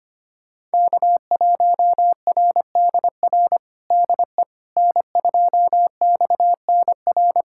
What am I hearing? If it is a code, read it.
Morse code, heard as a keyed tone, about 25 wpm, 700 Hz. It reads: K1RDR DE N2XNR
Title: USA Callsign DE Callsign - 5 Characters - 25wpm